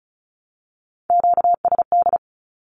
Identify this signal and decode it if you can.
Morse code, heard as a keyed tone, about 34 wpm, 700 Hz. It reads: QSD
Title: Q-Codes Repeated 3x 35wpm